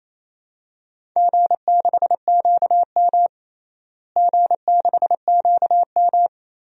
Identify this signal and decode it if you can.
Morse code, heard as a keyed tone, about 28 wpm, 700 Hz. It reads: G6QM G6QM
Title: DXCC-100 Call Signs - 4 Characters - Repeated 2x - 28wpm